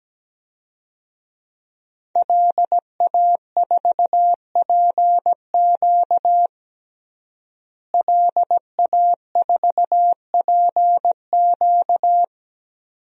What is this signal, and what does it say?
Morse code, heard as a keyed tone, about 17 wpm, 700 Hz. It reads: LA4PQ LA4PQ
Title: DXCC-100 Call Signs - 5 Characters - Repeated 2x - 17wpm